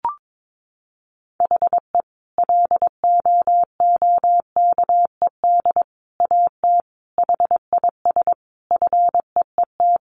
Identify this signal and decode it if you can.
Morse code, heard as a keyed tone, about 22 wpm, 700 Hz. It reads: HE LOOKED AT HIS FEET
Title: A Sound Of Thunder 22wpm